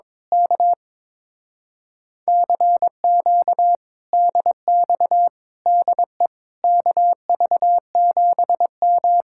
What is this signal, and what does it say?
Morse code, heard as a keyed tone, about 22 wpm, 700 Hz. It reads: K CQ DX DE KV7M
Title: Speed-Racing Common QSO Phrases - 22wpm